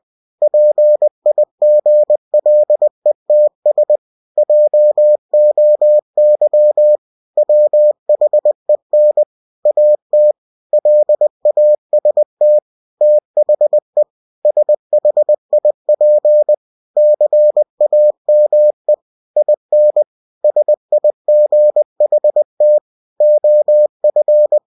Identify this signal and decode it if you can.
Morse code, heard as a keyed tone, about 20 wpm, 600 Hz. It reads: PIGLETS JOY WHEN AT LAST THE SHIP CAME IN SIGHT OF
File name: cw-books / winnie-the-pooh / winnie-20wpm-page-0100